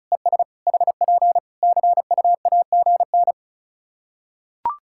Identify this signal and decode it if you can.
Morse code, heard as a keyed tone, about 34 wpm, 700 Hz. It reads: ES HP CUAGN
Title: Practice QSO - Exchange 4 - Ending - 35wpm